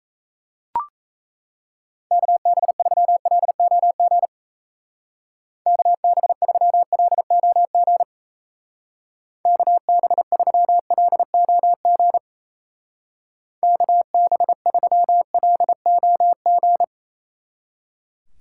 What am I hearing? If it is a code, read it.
Morse code, heard as a keyed tone, about 34 wpm, 700 Hz. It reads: KB3LOG KB3LOG KB3LOG KB3LOG
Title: Speed Racing - USA 2x3 Call Signs 28wpm